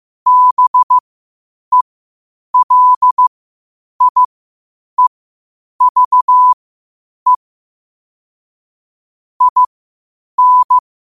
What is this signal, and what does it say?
Morse code, heard as a keyed tone, about 15 wpm, spaced as for 8 wpm, 1000 Hz. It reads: BELIEVE IN
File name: qotdfiles / 8wpm